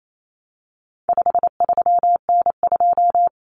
Morse code, heard as a keyed tone, about 28 wpm, 700 Hz. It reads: 53N2
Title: Common Cut Numbers - 4-Digit 28wpm